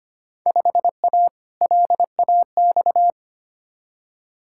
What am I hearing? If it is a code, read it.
Morse code, heard as a keyed tone, about 25 wpm, 700 Hz. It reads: 5A LAX
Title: Field Day Exchanges - 25wpm